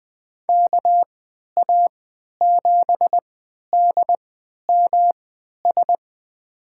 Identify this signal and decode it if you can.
Morse code, heard as a keyed tone, about 21 wpm, spaced as for 12 wpm, 700 Hz. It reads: KA7DMS
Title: USA Call Signs - 6 Characters - 3x Character Spacing - 20wpm